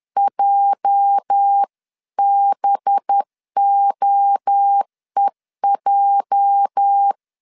Morse code, heard as a keyed tone, about 10 wpm, 800 Hz. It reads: JBOEJ
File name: LEZIONI / 8 YJ45 / 8 YJ 10wpm 2651